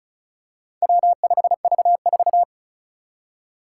Morse code, heard as a keyed tone, about 36 wpm, 700 Hz. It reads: W5V4
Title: Four Challenging Letter-Number Combinations 35wpm